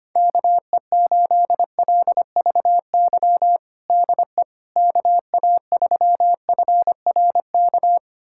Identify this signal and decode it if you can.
Morse code, heard as a keyed tone, about 25 wpm, 700 Hz. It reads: KE8LVY DE KA3FRK
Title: USA Callsign DE Callsign - 6 Characters - 25wpm